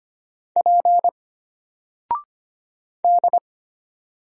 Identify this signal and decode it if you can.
Morse code, heard as a keyed tone, about 25 wpm, 700 Hz. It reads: P D
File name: Single-Letter-Number-25wpm